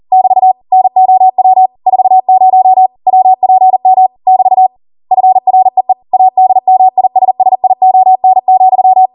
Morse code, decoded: <BT> NOW 40 WPM <BT> FREE ADMISSION,